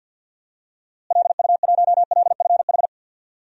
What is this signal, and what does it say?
Morse code, heard as a keyed tone, about 46 wpm, 700 Hz. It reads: LU1LFH
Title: DXCC-100 Call Signs - 6 Characters - Repeated 3x - 50wpm